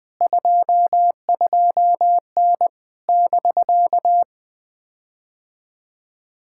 Morse code, heard as a keyed tone, about 20 wpm, 700 Hz. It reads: 22N <BK>
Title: POTA QSO Practice - 20wpm